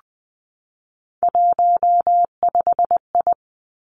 Morse code, heard as a keyed tone, about 20 wpm, 700 Hz. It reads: 15I
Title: Three Challenging Letter-Number Combinations 20wpm